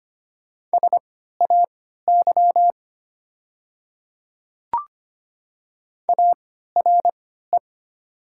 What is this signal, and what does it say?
Morse code, heard as a keyed tone, about 25 wpm, spaced as for 14 wpm, 690 Hz. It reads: SAY ARE
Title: 3x Character Spacing - Top 500 Words - 25wpm